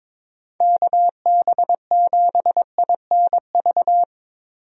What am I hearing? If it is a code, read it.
Morse code, heard as a keyed tone, about 23 wpm, 700 Hz. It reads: KB7INV
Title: USA Call Signs - 6 Characters - Repeated 2x - 22wpm